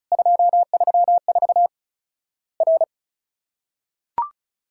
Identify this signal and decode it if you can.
Morse code, heard as a keyed tone, about 36 wpm, 700 Hz. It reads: 234 R
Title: Worked All Europe DX Contest - QTC Exchange - 35wpm